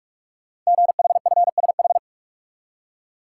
Morse code, heard as a keyed tone, about 44 wpm, 690 Hz. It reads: GHFSH
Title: Code Groups - 5 Letters - 45wpm